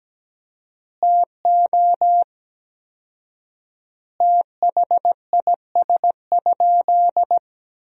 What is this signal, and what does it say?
Morse code, heard as a keyed tone, about 17 wpm, 700 Hz. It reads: TO THIS?
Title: 4x Word-Spacing - Sentences from Top 100 Words - 17wpm